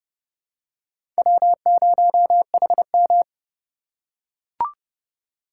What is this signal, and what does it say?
Morse code, heard as a keyed tone, about 29 wpm, 700 Hz. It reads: W0HM